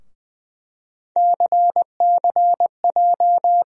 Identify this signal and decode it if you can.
Morse code, heard as a keyed tone, about 20 wpm, 700 Hz. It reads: CCJ
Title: USA Call Sign Postfixes - 3 Letters 20wpm